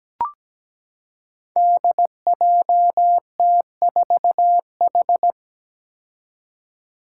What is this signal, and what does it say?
Morse code, heard as a keyed tone, about 17 wpm, 700 Hz. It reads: DJT4H
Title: Five Challenging Letter-Number Combinations 17wpm